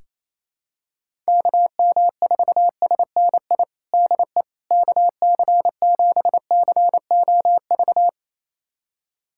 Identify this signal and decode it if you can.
Morse code, heard as a keyed tone, about 28 wpm, 700 Hz. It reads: KM4SNI DE KC7COV
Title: USA Callsign DE Callsign 28wpm